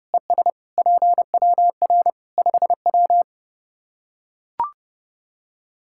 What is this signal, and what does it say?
Morse code, heard as a keyed tone, about 30 wpm, 700 Hz. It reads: ES PWR 5W
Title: Practice QSO - Exchange 2 - Info - 30wpm